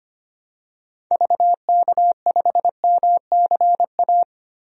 Morse code, heard as a keyed tone, about 25 wpm, 700 Hz. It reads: VK5MCA
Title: DXCC-100 Call Signs - 6 Characters - 25wpm